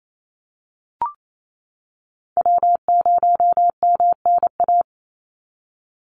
Morse code, heard as a keyed tone, about 28 wpm, 700 Hz. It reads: W0MNA
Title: NA SOTA Chaser Callsigns 28wpm